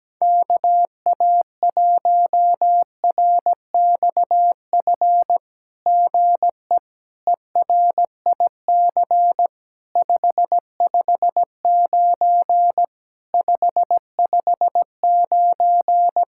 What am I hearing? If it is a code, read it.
Morse code, heard as a keyed tone, about 17 wpm, 700 Hz. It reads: KA1RXF GE ERIC 559 559
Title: SOTA QSO Practice 17wpm